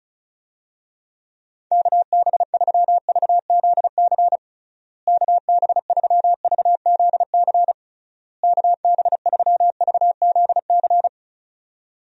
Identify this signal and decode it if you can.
Morse code, heard as a keyed tone, about 36 wpm, 700 Hz. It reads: KB3VZC KB3VZC KB3VZC
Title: USA Call Signs - 6 Characters - Repeated 3x - 35wpm